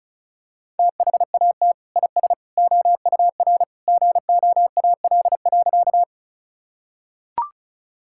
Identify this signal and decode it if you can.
Morse code, heard as a keyed tone, about 36 wpm, 700 Hz. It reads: THAT IS OUR GOAL.
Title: Sentences from Top 1000 Words - Review Part 5 - 35wpm